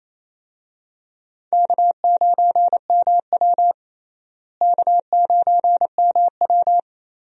Morse code, heard as a keyed tone, about 28 wpm, 700 Hz. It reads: K9MW K9MW
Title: USA Call Signs - 4 Characters - Repeated 2x - 28wpm